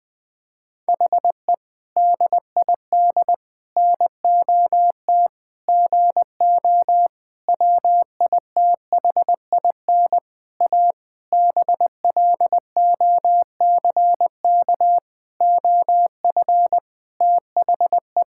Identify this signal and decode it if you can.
Morse code, heard as a keyed tone, about 20 wpm, 700 Hz. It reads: HE DID NOT GO WITHIN A BLOCK OF THE
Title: Sentences from Top 1600 Words 20wpm